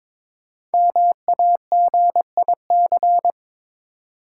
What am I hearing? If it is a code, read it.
Morse code, heard as a keyed tone, about 22 wpm, 690 Hz. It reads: MAGIC
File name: 5 Letter Words - First 3 Letters Given - 22wpm